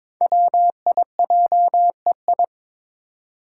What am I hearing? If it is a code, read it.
Morse code, heard as a keyed tone, about 22 wpm, 700 Hz. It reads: WIJEI
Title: Five Challenging Letter-Number Combinations 22wpm